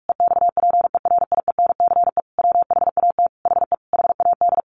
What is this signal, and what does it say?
Morse code, heard as a keyed tone, about 44 wpm, 700 Hz. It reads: EXPERIENCE WHAT HE HAD
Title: Sentences from Top 1100 Words 45wpm